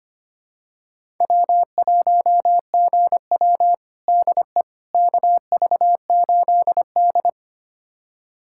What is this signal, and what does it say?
Morse code, heard as a keyed tone, about 25 wpm, 700 Hz. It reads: W1GW DE KV8D
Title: USA Callsign DE Callsign - 4 Characters - 25wpm